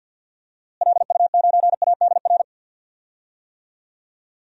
Morse code, heard as a keyed tone, about 53 wpm, 700 Hz. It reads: LU9ADR
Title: DXCC-100 Call Signs - 6 Characters - Repeated 2x - 50wpm